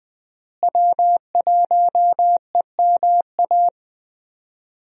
Morse code, heard as a keyed tone, about 21 wpm, 700 Hz. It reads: W1EMA
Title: USA Call Signs - 5 Characters - Repeated 3x - 20wpm